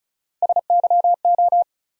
Morse code, heard as a keyed tone, about 36 wpm, 700 Hz. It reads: SYO
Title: USA Call Sign Postfixes - 3 Letters 35wpm